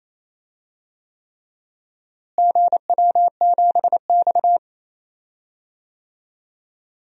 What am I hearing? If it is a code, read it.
Morse code, heard as a keyed tone, about 28 wpm, 700 Hz. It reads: GW7X
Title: DXCC-100 Call Signs - 4 Characters - Repeated 3x - 28wpm